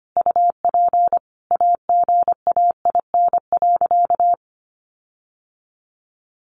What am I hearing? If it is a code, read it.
Morse code, heard as a keyed tone, about 25 wpm, 700 Hz. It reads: UP AGAIN.
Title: Sentences from Top 1400 Words 25wpm